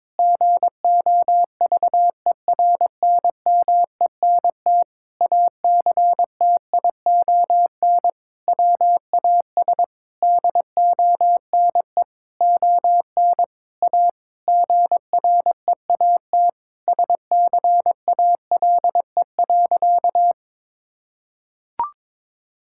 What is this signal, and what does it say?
Morse code, heard as a keyed tone, about 22 wpm, 700 Hz. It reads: GOVERNMENT ACTION WAS DONE ON A GREAT SCALE.
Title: Sentences from Top 1000 Words - Review Part 5 - 22wpm